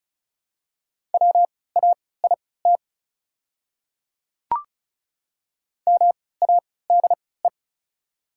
Morse code, heard as a keyed tone, about 36 wpm, spaced as for 20 wpm, 690 Hz. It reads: WAIT MADE
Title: 3x Character Spacing - Top 500 Words - 35wpm